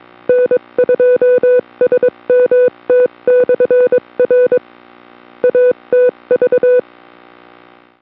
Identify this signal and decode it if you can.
Morse code, heard as a keyed tone, about 22 wpm, 500 Hz. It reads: N2SMT/R ATV
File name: kd2bd / atvcwid